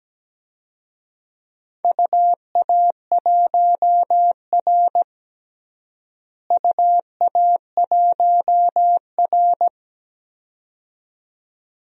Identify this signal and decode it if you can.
Morse code, heard as a keyed tone, about 17 wpm, 700 Hz. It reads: UA1R UA1R
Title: DXCC-100 Call Signs - 4 Characters - Repeated 2x - 17wpm